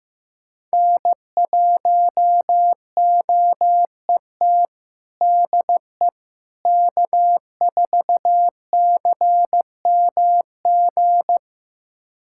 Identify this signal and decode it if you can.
Morse code, heard as a keyed tone, about 15 wpm, 700 Hz. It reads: N1OET DE K4CMG
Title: USA Callsign DE Callsign - 5 Characters - 15wpm